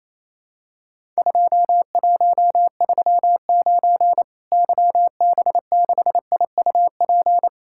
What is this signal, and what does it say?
Morse code, heard as a keyed tone, about 28 wpm, 700 Hz. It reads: 2139 YB6IUP
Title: Worked All Europe DX Contest - QTC Exchange - 28wpm